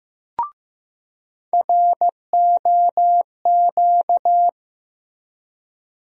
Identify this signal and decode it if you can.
Morse code, heard as a keyed tone, about 15 wpm, 700 Hz. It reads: ROQ
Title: USA Call Sign Postfixes - 3 Letters 15wpm